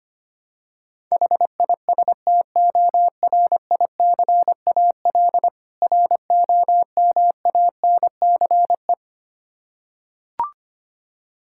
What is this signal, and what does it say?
Morse code, heard as a keyed tone, about 25 wpm, 700 Hz. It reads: HISTORICAL ROMANCE
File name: practice / 25wpm / Difficult Words with Hints - 25wpm